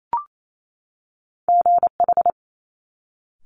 Morse code, heard as a keyed tone, about 28 wpm, 700 Hz. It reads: GH
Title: Top Two Letter Combinations in Words 28wpm